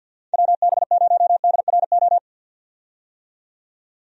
Morse code, heard as a keyed tone, about 51 wpm, 700 Hz. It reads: WB0DRO
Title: USA Call Signs - 6 Characters - 50wpm